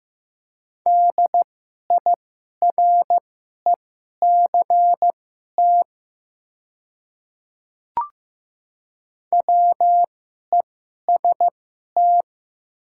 Morse code, heard as a keyed tone, about 15 wpm, spaced as for 11 wpm, 700 Hz. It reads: DIRECT WEST
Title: 2x Character Spacing - Top 500 Words - 15wpm